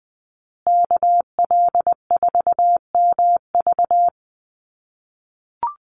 Morse code, heard as a keyed tone, about 20 wpm, 700 Hz. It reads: KL4MV
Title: USA Call Signs Repeated 2x 20wpm